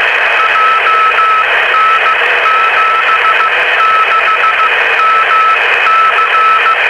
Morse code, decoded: JN76MC